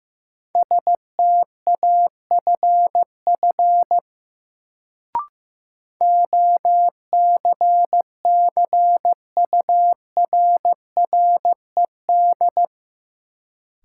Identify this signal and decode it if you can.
Morse code, heard as a keyed tone, about 15 wpm, 700 Hz. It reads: STAFF OCCURRED
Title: Sets of 5 Words Encore - New Word Drill - 15wpm